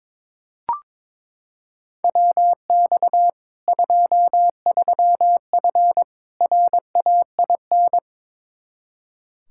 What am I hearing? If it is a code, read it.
Morse code, heard as a keyed tone, about 21 wpm, 700 Hz. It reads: WX 23F RAIN
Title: Practice QSO - Exchange 2 - Info - 22wpm